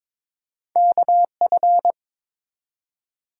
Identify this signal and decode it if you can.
Morse code, heard as a keyed tone, about 22 wpm, 700 Hz. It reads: KF
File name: practice / 22wpm / Two Letter-Number Combinations - 22wpm